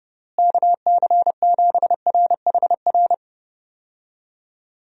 Morse code, heard as a keyed tone, about 30 wpm, 700 Hz. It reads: KC7RHR
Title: USA Call Signs - 6 Characters - Repeated 3x - 30wpm